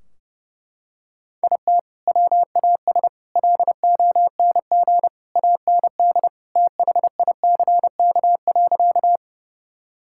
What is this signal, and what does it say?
Morse code, heard as a keyed tone, about 30 wpm, 700 Hz. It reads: IT WAS LONG AND THICK.
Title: Sentences from Top 1600 Words 30wpm